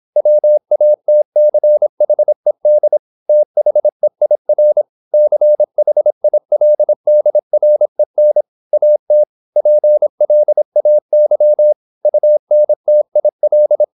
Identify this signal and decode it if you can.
Morse code, heard as a keyed tone, about 26 wpm, 600 Hz. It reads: WATCHED THEIR CHILDREN AT PLAY UNTIL
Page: Never 26wpm | CW med Gnister